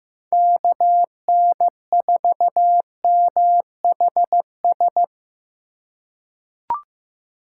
Morse code, heard as a keyed tone, about 15 wpm, 700 Hz. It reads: KN4MHS
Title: USA Call Signs - 2x Character Spacing - 15wpm